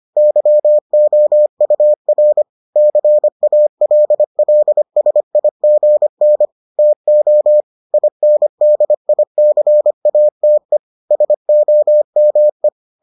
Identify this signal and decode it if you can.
Morse code, heard as a keyed tone, about 25 wpm, 600 Hz. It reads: YOUR CALLSIGN TO INDICATE SOME